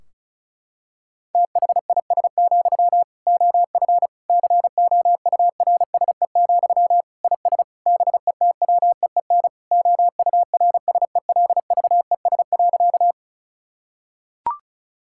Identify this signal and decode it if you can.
Morse code, heard as a keyed tone, about 34 wpm, 700 Hz. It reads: THIS, OF COURSE, IS BETWEEN OURSELVES.
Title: Sentences from Top 1600 Words 35wpm